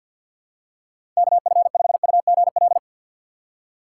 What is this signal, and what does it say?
Morse code, heard as a keyed tone, about 52 wpm, 690 Hz. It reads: KF5UGL
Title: USA Call Signs - 6 Characters - 8x Character Spacing - 50wpm